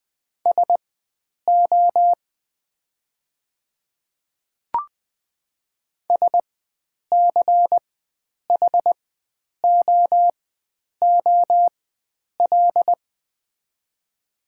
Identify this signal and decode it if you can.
Morse code, heard as a keyed tone, about 20 wpm, spaced as for 9 wpm, 700 Hz. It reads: SO SCHOOL
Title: 4x Character Spacing - Top 500 Words - 20wpm